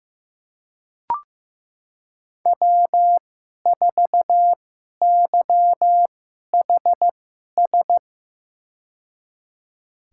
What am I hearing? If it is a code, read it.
Morse code, heard as a keyed tone, about 15 wpm, spaced as for 11 wpm, 700 Hz. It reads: W4YHS
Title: USA Call Signs - 2x Character Spacing - 15wpm